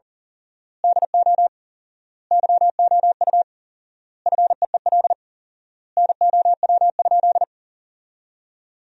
Morse code, heard as a keyed tone, about 40 wpm, 700 Hz. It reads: DO YOU FEEL NOW?
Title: 4x Word-Spacing - Sentences from Top 400 Words - 40wpm